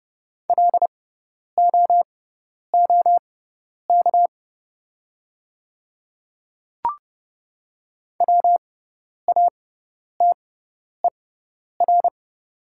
Morse code, heard as a keyed tone, about 30 wpm, spaced as for 10 wpm, 700 Hz. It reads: LOOK WATER
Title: 6x Character Spacing - Top 100 Words - 30wpm